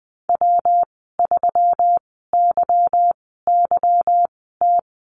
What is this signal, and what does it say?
Morse code, heard as a keyed tone, about 20 wpm, spaced as for 15 wpm, 700 Hz. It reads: W3YYT
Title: 186 Lesson 30g - Letter b - Call Signs 20wpm